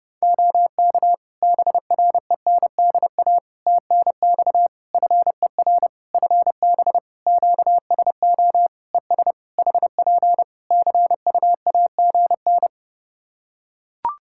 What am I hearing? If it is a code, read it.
Morse code, heard as a keyed tone, about 28 wpm, 700 Hz. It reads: OK BRENDA TNX FER FB QSO ES HP CUAGN
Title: Practice QSO - Exchange 4 - Ending - 30wpm